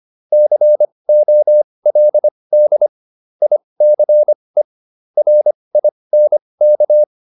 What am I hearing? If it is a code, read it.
Morse code, heard as a keyed tone, about 25 wpm, spaced as for 20 wpm, 600 Hz. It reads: COLD ICE RINK
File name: PR301 20